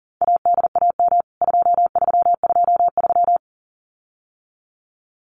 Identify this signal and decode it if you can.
Morse code, heard as a keyed tone, about 37 wpm, 700 Hz. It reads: ADAM 2323
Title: CWT Exchange 40wpm